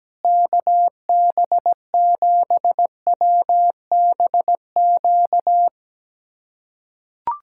Morse code, heard as a keyed tone, about 17 wpm, 700 Hz. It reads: KB7WBQ
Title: USA Callsign DE Callsign - 6 Characters - 17wpm